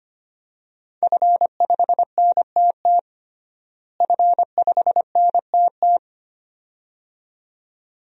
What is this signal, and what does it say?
Morse code, heard as a keyed tone, about 25 wpm, 700 Hz. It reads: F5NTT F5NTT
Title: DXCC-100 Call Signs - 5 Characters - Repeated 2x - 25wpm